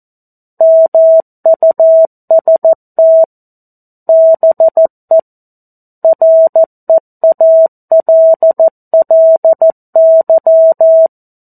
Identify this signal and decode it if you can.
Morse code, heard as a keyed tone, about 14 wpm, 650 Hz. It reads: MUST BE REALLY
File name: 02-Chapter-02 14 WPM